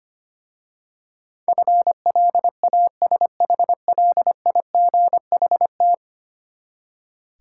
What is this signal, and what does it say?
Morse code, heard as a keyed tone, about 25 wpm, 700 Hz. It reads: FLASHLIGHT
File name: practice / 25wpm / Easy Words with Hints - 25wpm